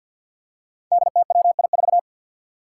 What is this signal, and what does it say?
Morse code, heard as a keyed tone, about 49 wpm, 700 Hz. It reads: DTWI4
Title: Five Challenging Letter-Number Combinations 50wpm